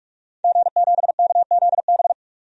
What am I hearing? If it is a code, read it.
Morse code, heard as a keyed tone, about 46 wpm, 700 Hz. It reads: G7KZB
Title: DXCC-100 Call Signs - 5 Characters - 45wpm